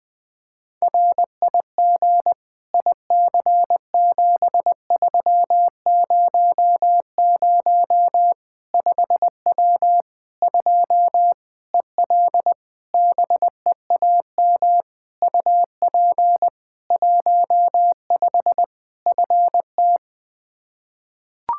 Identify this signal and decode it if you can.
Morse code, heard as a keyed tone, about 20 wpm, 700 Hz. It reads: RIG IC7300 5W 2 EL BEAM UP 15 FT
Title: K1USN SST QSO Practice 20wpm